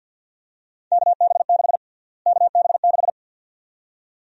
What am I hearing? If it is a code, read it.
Morse code, heard as a keyed tone, about 50 wpm, 700 Hz. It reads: KB6 KB6
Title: USA Call Sign Prefixes - Repeated 2x - 50wpm